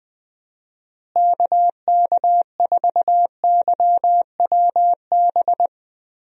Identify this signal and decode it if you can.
Morse code, heard as a keyed tone, about 20 wpm, 700 Hz. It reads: KK4YWB
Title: USA Call Signs - 6 Characters - Repeated 3x - 20wpm